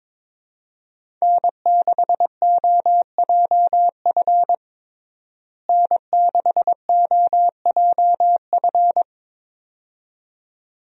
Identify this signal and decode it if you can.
Morse code, heard as a keyed tone, about 22 wpm, 700 Hz. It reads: N6OJF N6OJF
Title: USA Call Signs - 5 Characters - Repeated 2x - 22wpm